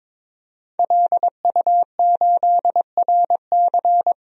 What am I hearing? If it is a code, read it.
Morse code, heard as a keyed tone, about 22 wpm, 700 Hz. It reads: LU8RC
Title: DXCC-100 Call Signs - 5 Characters - 22wpm